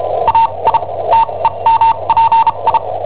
Morse code, heard as a keyed tone, about 34 wpm, spaced as for 26 wpm, 910 Hz. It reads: AI TEMPI
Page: Cassetta esercizi preparata da I7OHP per IW5ELP